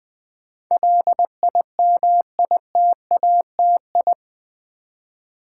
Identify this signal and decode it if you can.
Morse code, heard as a keyed tone, about 20 wpm, 700 Hz. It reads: LIMITATI
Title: Build A Word - 4 Letters At A Time - 10-Letter Words - 20wpm